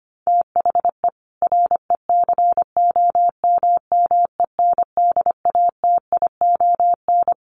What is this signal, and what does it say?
Morse code, heard as a keyed tone, about 25 wpm, 700 Hz. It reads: THE RECOMMENDATION
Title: Sentences from Top 1400 Words 25wpm